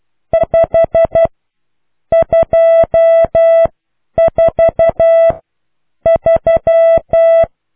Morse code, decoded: 5 24 3